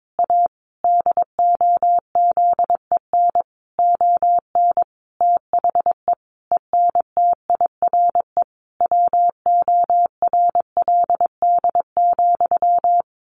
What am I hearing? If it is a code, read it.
Morse code, heard as a keyed tone, about 22 wpm, 700 Hz. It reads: A DOZEN ON THE ENTIRE WORLD,
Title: A Sound Of Thunder 22wpm